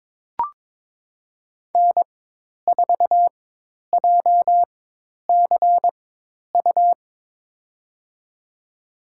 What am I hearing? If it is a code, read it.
Morse code, heard as a keyed tone, about 22 wpm, spaced as for 10 wpm, 700 Hz. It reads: N4JCU
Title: USA Call Signs - 5 Characters - 4x Character Spacing - 22wpm